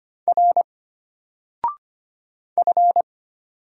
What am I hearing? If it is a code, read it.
Morse code, heard as a keyed tone, about 25 wpm, 700 Hz. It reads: R F
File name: Single-Letter-Number-25wpm